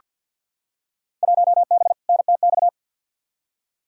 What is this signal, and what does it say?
Morse code, heard as a keyed tone, about 47 wpm, 700 Hz. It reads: 1B NTX
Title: Field Day Exchanges - 50wpm